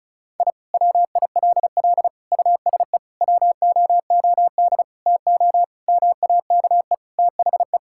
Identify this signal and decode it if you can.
Morse code, heard as a keyed tone, about 36 wpm, 700 Hz. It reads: I WILL USE WOOD TO MAKE THE
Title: Sentences from Top 1000 Words - Review Part 6 - 35wpm